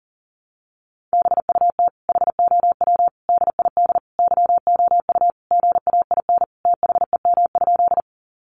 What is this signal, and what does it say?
Morse code, heard as a keyed tone, about 40 wpm, 700 Hz. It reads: BUT HOW DID YOU GAIN THEM?
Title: Sentences from Top 1000 Words 40wpm